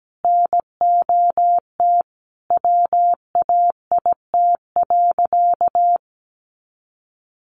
Morse code, heard as a keyed tone, about 17 wpm, 700 Hz. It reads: NOT WAIT.
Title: Sentences from Top 1400 Words 17wpm